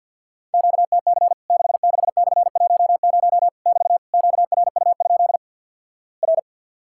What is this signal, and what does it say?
Morse code, heard as a keyed tone, about 50 wpm, 700 Hz. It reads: QTC 66/10 <BT> QRU? R
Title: Worked All Europe DX Contest - QTC Exchange with Repeats - 50wpm